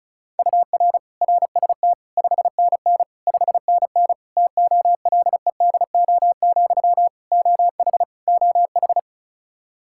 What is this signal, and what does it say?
Morse code, heard as a keyed tone, about 36 wpm, 700 Hz. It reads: UR RST 5NN 5NN TOLEDO, OH OH
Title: Practice QSO - Exchange 1 - Beginning - 35wpm